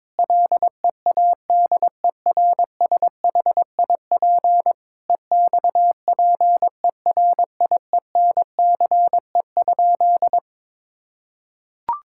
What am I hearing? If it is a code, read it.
Morse code, heard as a keyed tone, about 21 wpm, 700 Hz. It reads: LEADERSHIP EXPERIENCE?
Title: Sentences from Top 1600 Words 22wpm